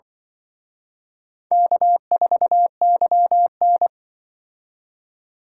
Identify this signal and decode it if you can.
Morse code, heard as a keyed tone, about 24 wpm, 700 Hz. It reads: K4YN
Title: Speed Racing - USA Call Signs 20wpm